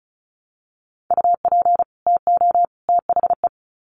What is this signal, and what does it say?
Morse code, heard as a keyed tone, about 34 wpm, 700 Hz. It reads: UP TO THE
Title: Sets of 3 Words 35wpm